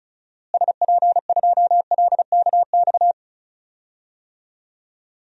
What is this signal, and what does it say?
Morse code, heard as a keyed tone, about 36 wpm, 700 Hz. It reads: SP2LKX
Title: DXCC-100 Call Signs - 6 Characters - Repeated 3x - 35wpm